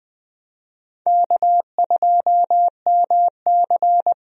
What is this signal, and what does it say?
Morse code, heard as a keyed tone, about 20 wpm, 700 Hz. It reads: K2MC
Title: USA Call Signs - 4 Characters - Repeated 2x - 20wpm